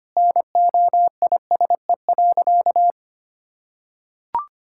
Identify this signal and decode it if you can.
Morse code, heard as a keyed tone, about 25 wpm, 700 Hz. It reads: NOISE.
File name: practice / 25wpm / Sentences with 2-Word Phrases - 25wpm